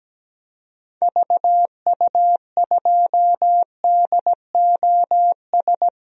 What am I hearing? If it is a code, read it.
Morse code, heard as a keyed tone, about 17 wpm, 700 Hz. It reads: VU2DOS
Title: DXCC-100 Call Signs - 6 Characters - Repeated 2x - 17wpm